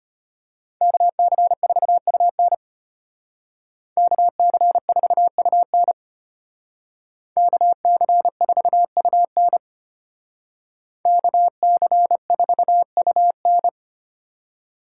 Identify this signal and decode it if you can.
Morse code, heard as a keyed tone, about 33 wpm, 700 Hz. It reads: KC4UN KC4UN KC4UN KC4UN